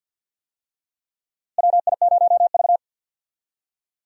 Morse code, heard as a keyed tone, about 46 wpm, 700 Hz. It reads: WI0V
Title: USA Call Signs - 4 Characters - Repeated 2x - 50wpm